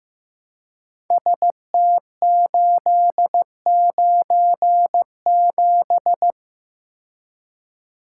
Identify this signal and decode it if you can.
Morse code, heard as a keyed tone, about 15 wpm, 700 Hz. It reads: ST897
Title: Code Groups - 5 Letter:Numbers - 15wpm